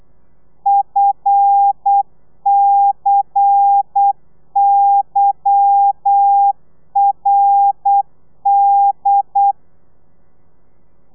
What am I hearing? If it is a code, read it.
Morse code, heard as a keyed tone, about 8 wpm, 800 Hz. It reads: FCYRD
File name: telegrafia / i2tfj / Lez10 35